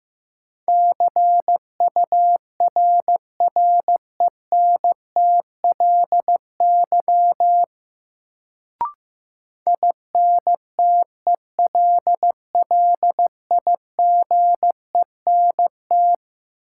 Morse code, heard as a keyed tone, about 15 wpm, 700 Hz. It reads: CURRENTLY INTELLIGENT
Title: Sentences from Top 1000 Words - Review Part 7 - New Word Drill - 15wpm